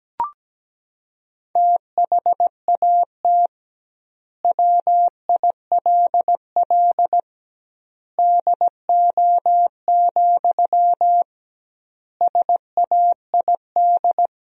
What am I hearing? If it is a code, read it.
Morse code, heard as a keyed tone, about 17 wpm, 700 Hz. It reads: THAT WILL DO, SAID
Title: 2x Word-Spacing - Sentences from Top 100 Words - 17wpm